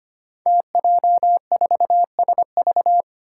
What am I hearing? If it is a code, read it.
Morse code, heard as a keyed tone, about 25 wpm, 700 Hz. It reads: TJ4SV
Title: Five Challenging Letter-Number Combinations 25wpm